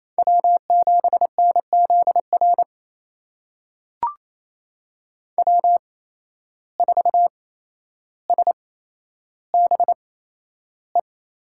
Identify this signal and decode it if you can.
Morse code, heard as a keyed tone, about 28 wpm, spaced as for 12 wpm, 700 Hz. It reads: W7NZR W4SBE
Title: USA Call Signs - 5 Characters - 8x Character Spacing - 28wpm